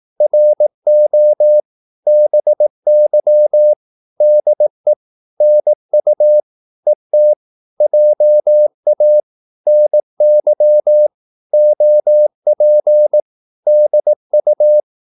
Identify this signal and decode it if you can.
Morse code, heard as a keyed tone, about 18 wpm, 600 Hz. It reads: RO BY DE NU ET JA NY OP DU
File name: Korte ord DK 18wpm